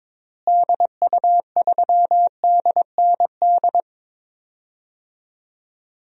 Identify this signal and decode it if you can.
Morse code, heard as a keyed tone, about 21 wpm, 700 Hz. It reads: DU3DND
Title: DXCC-100 Call Signs - 6 Characters - Repeated 2x - 22wpm